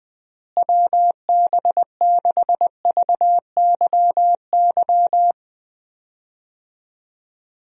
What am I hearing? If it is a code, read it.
Morse code, heard as a keyed tone, about 20 wpm, 700 Hz. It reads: WB6VYY
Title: USA Call Signs - 6 Characters - Repeated 2x - 20wpm